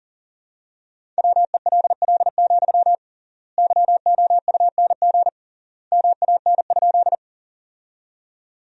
Morse code, heard as a keyed tone, about 40 wpm, 700 Hz. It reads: WELL, YOUNG MAN?
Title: 3x Word-Spacing - Sentences from Top 400 Words - 40wpm